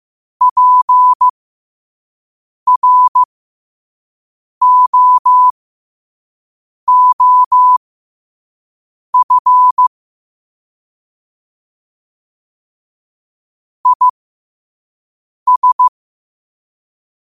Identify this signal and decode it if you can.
Morse code, heard as a keyed tone, about 15 wpm, spaced as for 5 wpm, 1000 Hz. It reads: PROOF IS
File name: qotdfiles / 5wpm